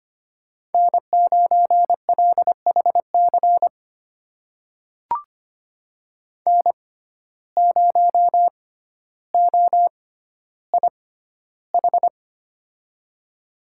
Morse code, heard as a keyed tone, about 25 wpm, spaced as for 13 wpm, 700 Hz. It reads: N9LHC N0OIH
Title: USA Call Signs - 5 Characters - 6x Character Spacing - 25wpm